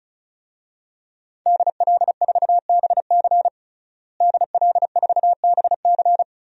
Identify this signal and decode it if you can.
Morse code, heard as a keyed tone, about 36 wpm, 700 Hz. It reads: DL4BC DL4BC
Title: DXCC-100 Call Signs - 5 Characters - Repeated 2x - 35wpm